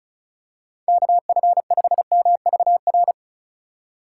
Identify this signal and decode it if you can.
Morse code, heard as a keyed tone, about 36 wpm, 700 Hz. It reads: KF5MVR
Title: USA Call Signs - 6 Characters - 4x Character Spacing - 35wpm